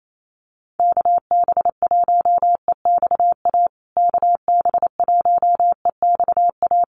Morse code, heard as a keyed tone, about 28 wpm, 700 Hz. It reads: KB1EXA KB1EXA
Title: USA Call Signs Repeated 3x 28wpm